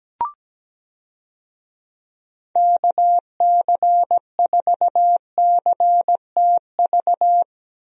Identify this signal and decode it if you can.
Morse code, heard as a keyed tone, about 17 wpm, 700 Hz. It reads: KC4CTV
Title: USA Call Signs - 6 Characters - Repeated 2x - 17wpm